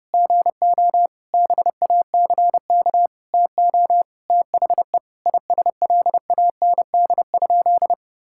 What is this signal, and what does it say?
Morse code, heard as a keyed tone, about 30 wpm, 700 Hz. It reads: GO BACK TO THE ISLAND?
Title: Sentences from Top 1600 Words 30wpm